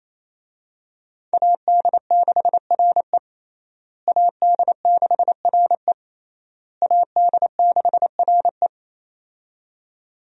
Morse code, heard as a keyed tone, about 28 wpm, 700 Hz. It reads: AD6RE AD6RE AD6RE
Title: USA Call Signs - 5 Characters - Repeated 3x - 28wpm